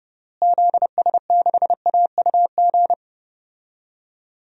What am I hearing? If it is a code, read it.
Morse code, heard as a keyed tone, about 30 wpm, 700 Hz. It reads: ZS6AUG
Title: DXCC-100 Call Signs - 6 Characters - Repeated 2x - 30wpm